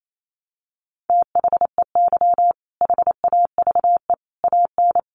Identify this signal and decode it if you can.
Morse code, heard as a keyed tone, about 28 wpm, 700 Hz. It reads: THEY HAVE AN
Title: Sentences from Top 1000 Words 28wpm